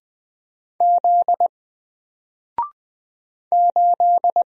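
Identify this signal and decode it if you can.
Morse code, heard as a keyed tone, about 20 wpm, 700 Hz. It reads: Z 8
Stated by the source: Single letter and number Morse drill at 20 wpm, from N4LCD.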